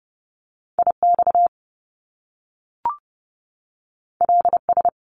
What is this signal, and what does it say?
Morse code, heard as a keyed tone, about 30 wpm, 700 Hz. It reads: IX LS
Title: Top Two Letter Combinations in Words 30wpm